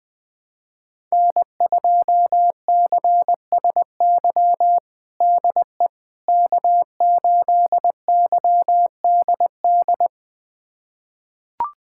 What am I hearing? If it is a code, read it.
Morse code, heard as a keyed tone, about 20 wpm, 700 Hz. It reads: N2CSY DE K8YDD
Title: USA Callsign DE Callsign - 5 Characters - 20wpm